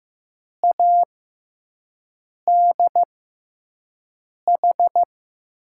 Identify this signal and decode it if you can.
Morse code, heard as a keyed tone, about 16 wpm, spaced as for 5 wpm, 700 Hz. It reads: ADH
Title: USA Call Signs - 6 Characters - 6x Character Spacing - 15wpm